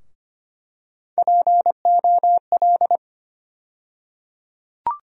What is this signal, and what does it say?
Morse code, heard as a keyed tone, about 25 wpm, 700 Hz. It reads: POL
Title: USA Call Sign Postfixes - 3 Letters 25wpm